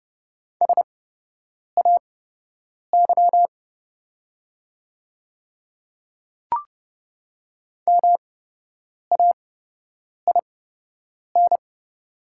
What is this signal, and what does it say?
Morse code, heard as a keyed tone, about 30 wpm, spaced as for 8 wpm, 690 Hz. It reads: SAY MAIN
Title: 8x Character Spacing - Top 500 Words - 30wpm